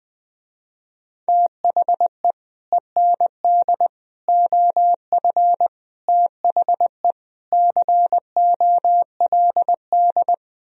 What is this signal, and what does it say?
Morse code, heard as a keyed tone, about 20 wpm, 700 Hz. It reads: THE END OF THE COLD
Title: Speed-Racing - Sets of 5 Words 17wpm